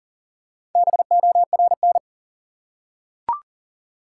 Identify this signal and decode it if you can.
Morse code, heard as a keyed tone, about 40 wpm, 700 Hz. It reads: BORN
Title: Four Letter Words - 40wpm